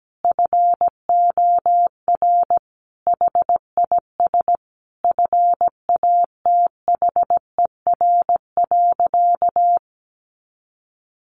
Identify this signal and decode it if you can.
Morse code, heard as a keyed tone, about 17 wpm, 700 Hz. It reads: FOR HIS FATHER.
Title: Sentences from Top 500 Words 17wpm